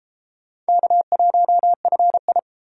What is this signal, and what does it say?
Morse code, heard as a keyed tone, about 33 wpm, 700 Hz. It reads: K1FI